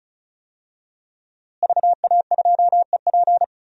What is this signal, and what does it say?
Morse code, heard as a keyed tone, about 34 wpm, 700 Hz. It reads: VA2EP
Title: DXCC-100 Call Signs - 5 Characters - Repeated 2x - 35wpm